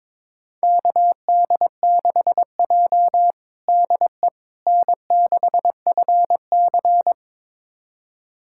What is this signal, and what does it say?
Morse code, heard as a keyed tone, about 22 wpm, 700 Hz. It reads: KD6J DE N6FC
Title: USA Callsign DE Callsign - 4 Characters - 22wpm